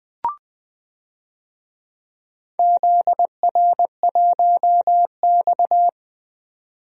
Morse code, heard as a keyed tone, about 20 wpm, 700 Hz. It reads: ZR1X
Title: DXCC-100 Call Signs - 4 Characters - Repeated 3x - 20wpm